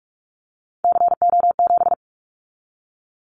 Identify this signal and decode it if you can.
Morse code, heard as a keyed tone, about 44 wpm, 700 Hz. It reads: CO7
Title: Callsign Prefixes for Top 100 DXCC - 45wpm